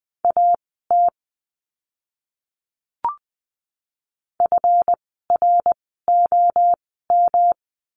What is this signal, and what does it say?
Morse code, heard as a keyed tone, about 20 wpm, spaced as for 15 wpm, 700 Hz. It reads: AT FROM
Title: 139 Lesson 23f - Letter f - Words 20wpm